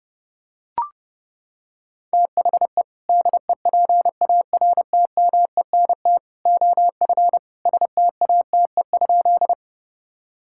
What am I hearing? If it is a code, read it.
Morse code, heard as a keyed tone, about 32 wpm, 700 Hz. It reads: THE DEPARTMENT OF STATE?
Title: Sentences from Top 1000 Words - Review Part 5 - 30wpm